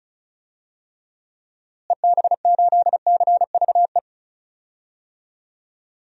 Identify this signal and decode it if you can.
Morse code, heard as a keyed tone, about 36 wpm, 700 Hz. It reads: EB8CVE
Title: DXCC-100 Call Signs - 6 Characters - Repeated 3x - 35wpm